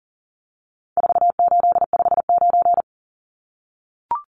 Morse code, heard as a keyed tone, about 40 wpm, 700 Hz. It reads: 4859
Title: Numbers - 4-Digit 40wpm